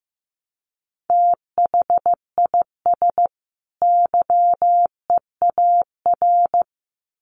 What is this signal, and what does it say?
Morse code, heard as a keyed tone, about 15 wpm, 700 Hz. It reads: THIS YEAR
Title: Sets of 2 Words 15wpm